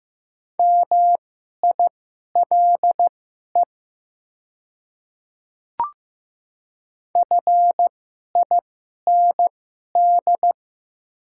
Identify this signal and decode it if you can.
Morse code, heard as a keyed tone, about 15 wpm, spaced as for 11 wpm, 700 Hz. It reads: MILE FIND
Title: 2x Character Spacing - Top 300 Words - 15wpm